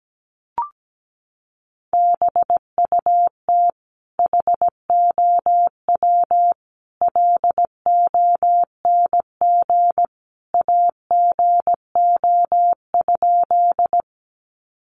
Morse code, heard as a keyed tone, about 17 wpm, 700 Hz. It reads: BUT HOW LONG AGO?
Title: Sentences from Top 500 Words 17wpm